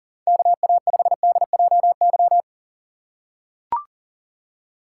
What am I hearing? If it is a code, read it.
Morse code, heard as a keyed tone, about 40 wpm, 700 Hz. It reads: KA5DJY
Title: USA Call Signs - 6 Characters - 2x Character Spacing - 40wpm